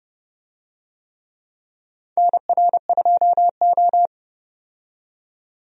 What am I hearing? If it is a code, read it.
Morse code, heard as a keyed tone, about 30 wpm, 700 Hz. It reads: NR2O
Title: USA Call Signs - 4 Characters - 30wpm